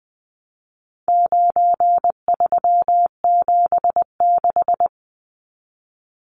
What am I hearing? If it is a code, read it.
Morse code, heard as a keyed tone, about 20 wpm, 700 Hz. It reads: 9376
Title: Numbers - 4-Digit 20wpm